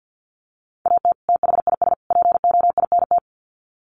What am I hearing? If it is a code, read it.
Morse code, heard as a keyed tone, about 48 wpm, 700 Hz. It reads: AT THIS POINT
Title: Sets of 3 Words 50wpm